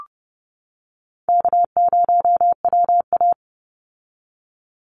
Morse code, heard as a keyed tone, about 30 wpm, 700 Hz. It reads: K0WA
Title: Easy Call Signs - 30wpm